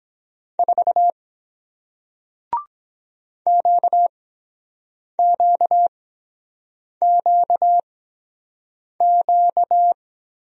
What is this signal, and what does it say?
Morse code, heard as a keyed tone, about 23 wpm, spaced as for 7 wpm, 700 Hz. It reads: 4 QQQQ